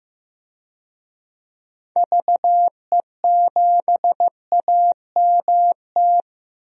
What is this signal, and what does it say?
Morse code, heard as a keyed tone, about 16 wpm, 700 Hz. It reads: VE7AMT
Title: DXCC-100 Call Signs - 6 Characters - Repeated 2x - 15wpm